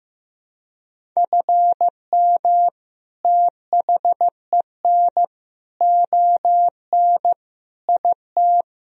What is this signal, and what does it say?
Morse code, heard as a keyed tone, about 15 wpm, 700 Hz. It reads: FM THEN ON IT
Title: Sentences from Top 400 Words and Abbreviations 15wpm